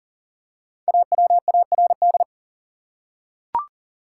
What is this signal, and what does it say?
Morse code, heard as a keyed tone, about 40 wpm, 700 Hz. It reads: AWARD
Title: Five Letter Words - 40wpm